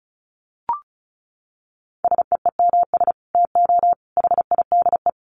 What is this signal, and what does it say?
Morse code, heard as a keyed tone, about 34 wpm, 700 Hz. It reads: SEEMS TO HIDE
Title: Dit Challenge 35wpm